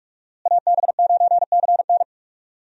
Morse code, heard as a keyed tone, about 45 wpm, 700 Hz. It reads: AB9CN
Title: USA Call Signs - 5 Characters - Repeated 3x - 45wpm